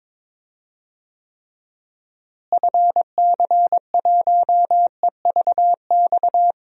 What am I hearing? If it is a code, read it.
Morse code, heard as a keyed tone, about 21 wpm, 700 Hz. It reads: FC1EVX
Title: DXCC-100 Call Signs - 6 Characters - Repeated 3x - 22wpm